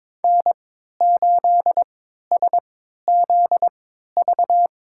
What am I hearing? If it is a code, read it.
Morse code, heard as a keyed tone, about 22 wpm, spaced as for 13 wpm, 700 Hz. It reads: N8SZV
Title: USA Call Signs - 5 Characters - 3x Character Spacing - 22wpm